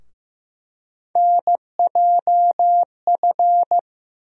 Morse code, heard as a keyed tone, about 15 wpm, 700 Hz. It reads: NJF
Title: USA Call Sign Postfixes - 3 Letters 15wpm